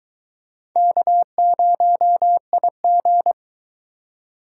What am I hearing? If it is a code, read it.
Morse code, heard as a keyed tone, about 22 wpm, 700 Hz. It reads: K0IG